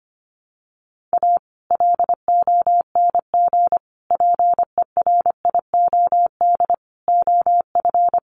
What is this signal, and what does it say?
Morse code, heard as a keyed tone, about 25 wpm, 700 Hz. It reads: A LONG PERIOD OF
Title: Sets of 5 Words 25wpm